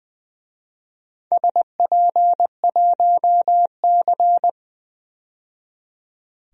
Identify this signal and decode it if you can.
Morse code, heard as a keyed tone, about 20 wpm, 700 Hz. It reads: SP1C
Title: DXCC-100 Call Signs - 4 Characters - 20wpm